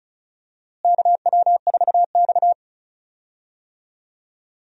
Morse code, heard as a keyed tone, about 34 wpm, 700 Hz. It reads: KW4X
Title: USA Call Signs - 4 Characters - Repeated 2x - 35wpm